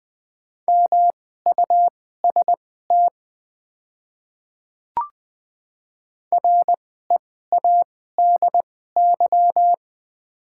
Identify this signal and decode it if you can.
Morse code, heard as a keyed tone, about 20 wpm, spaced as for 15 wpm, 700 Hz. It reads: MUST READY
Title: 2x Character Spacing - Top 500 Words - 20wpm